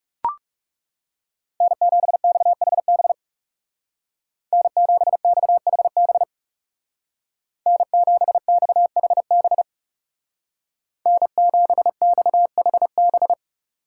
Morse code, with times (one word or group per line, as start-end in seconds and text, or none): 1.6-3.1 N7XHB
4.5-6.2 N7XHB
7.7-9.6 N7XHB
11.1-13.3 N7XHB